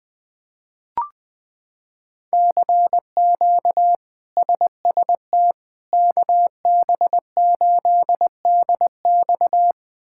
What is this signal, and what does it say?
Morse code, heard as a keyed tone, about 21 wpm, 700 Hz. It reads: CQ SST KB8DX
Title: K1USN SST QSO Practice - 20wpm